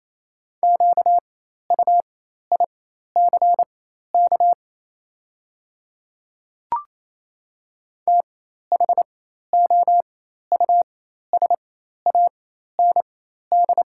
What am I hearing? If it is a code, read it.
Morse code, heard as a keyed tone, about 28 wpm, spaced as for 13 wpm, 690 Hz. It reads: QUICK THOUSAND
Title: 4x Character Spacing - Top 500 Words - 28wpm